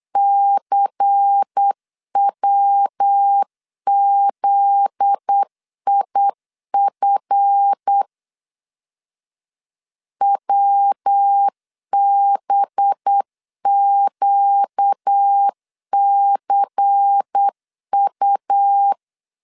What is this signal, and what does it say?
Morse code, heard as a keyed tone, about 8 wpm, 800 Hz. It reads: CWZIF WBQCU
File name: LEZIONI / 7 GF9Z / 7 GF9Z 8wpm 3213